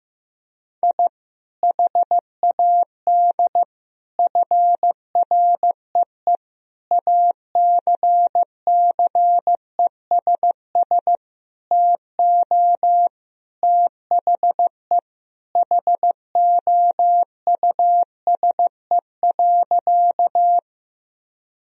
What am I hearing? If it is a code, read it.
Morse code, heard as a keyed tone, about 15 wpm, 700 Hz. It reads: I HAD FREE ACCESS TO THE HOUSE.
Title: Sentences from Top 1000 Words - Review Part 4 - 15wpm